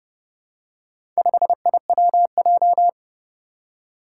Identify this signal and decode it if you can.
Morse code, heard as a keyed tone, about 32 wpm, 700 Hz. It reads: 5IWJ
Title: Four Challenging Letter-Number Combinations 30wpm